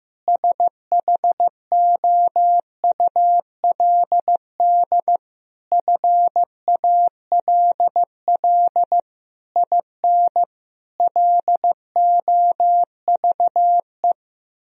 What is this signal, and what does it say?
Morse code, heard as a keyed tone, about 15 wpm, 700 Hz. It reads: SHOULD FALL IN LOVE
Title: Sentences from Top 1000 Words - Review Part 4 - 15wpm